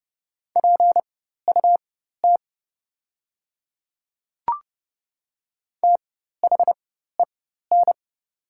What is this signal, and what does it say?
Morse code, heard as a keyed tone, about 30 wpm, spaced as for 14 wpm, 700 Hz. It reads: PUT THEN
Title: 4x Character Spacing - Top 300 Words - 30wpm